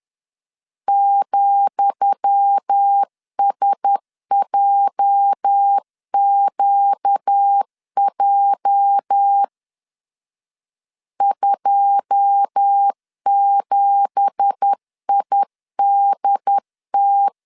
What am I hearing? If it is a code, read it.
Morse code, heard as a keyed tone, about 10 wpm, 800 Hz. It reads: ,SJQJ 27IDT
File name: LEZIONI / 10 SIGNS / 10D COMA 10wpm 3013